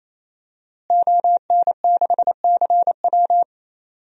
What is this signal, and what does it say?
Morse code, heard as a keyed tone, about 28 wpm, 700 Hz. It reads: ON6CW
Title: DXCC-100 Call Signs - 5 Characters - Repeated 3x - 28wpm